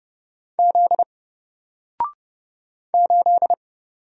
Morse code, heard as a keyed tone, about 31 wpm, 700 Hz. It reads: Z 8
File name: Single-Letter-Number-30wpm